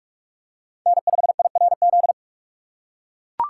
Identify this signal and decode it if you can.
Morse code, heard as a keyed tone, about 44 wpm, 700 Hz. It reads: N5IRZ
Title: USA Call Signs - 6x Character Spacing - 45wpm